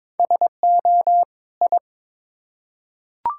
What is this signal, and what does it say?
Morse code, heard as a keyed tone, about 21 wpm, 700 Hz. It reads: SO I
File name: Build A Phrase - 2-Word Phrases - 22wpm